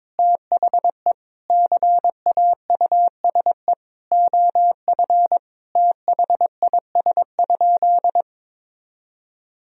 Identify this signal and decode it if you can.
Morse code, heard as a keyed tone, about 22 wpm, 700 Hz. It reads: THE CAUSE OF THIS?
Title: Sentences from Top 1000 Words - Review Part 1 - 22wpm